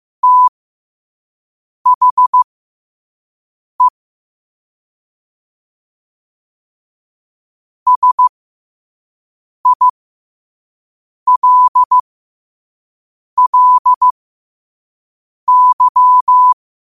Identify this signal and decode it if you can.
Morse code, heard as a keyed tone, about 15 wpm, spaced as for 5 wpm, 1000 Hz. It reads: THE SILLY
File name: qotdfiles / 5wpm